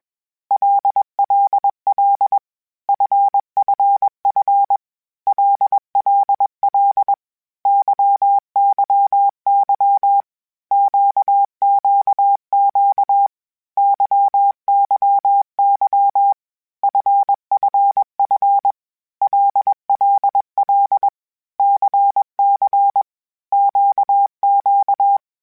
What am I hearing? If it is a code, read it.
Morse code, heard as a keyed tone, about 21 wpm, 800 Hz. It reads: LLL FFF LLL YYY QQQ YYY FFF LLL CC QQ
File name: Basic Sets / Basic Letter Groups Collective7 / LFCYQ@20